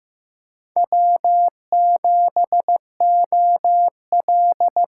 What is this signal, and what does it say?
Morse code, heard as a keyed tone, about 15 wpm, 700 Hz. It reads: W7OL
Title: USA Call Signs - 5 Characters - 8x Character Spacing - 15wpm